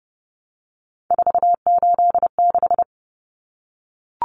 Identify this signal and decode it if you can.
Morse code, heard as a keyed tone, about 30 wpm, 700 Hz. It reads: 486
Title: Common Cut Numbers - 3-Digit 30wpm